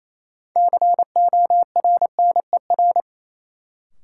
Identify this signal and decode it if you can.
Morse code, heard as a keyed tone, about 28 wpm, 700 Hz. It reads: CORNER
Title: Sets of 5 Words - New Word Drill - 28wpm